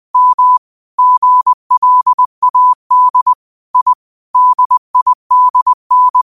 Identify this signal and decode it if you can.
Morse code, heard as a keyed tone, about 20 wpm, 1000 Hz. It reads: M GLAD I DIDN
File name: qotdfiles / 20wpm